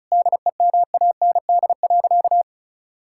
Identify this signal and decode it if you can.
Morse code, heard as a keyed tone, about 36 wpm, 690 Hz. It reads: DEMAND.
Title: Sentences from Top 1000 Words - Review Part 5 - 35wpm